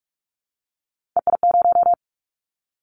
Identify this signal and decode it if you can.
Morse code, heard as a keyed tone, about 44 wpm, 700 Hz. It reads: EI0
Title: Callsign Prefixes for Top 50 DXCC - 45wpm